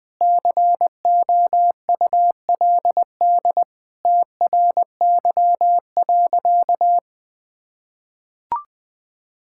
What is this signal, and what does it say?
Morse code, heard as a keyed tone, about 20 wpm, 700 Hz. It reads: COULD TRY.
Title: Sentences with 2-Word Phrases - 20wpm